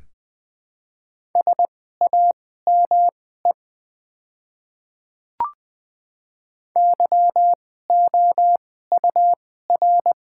Morse code, heard as a keyed tone, about 20 wpm, spaced as for 14 wpm, 700 Hz. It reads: SAME YOUR
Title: 2x Character Spacing - Top 300 Words - 20wpm